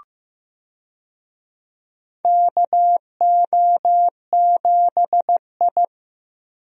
Morse code, heard as a keyed tone, about 15 wpm, 700 Hz. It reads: KO7I
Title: USA Call Signs - 4 Characters - Repeated 2x - 15wpm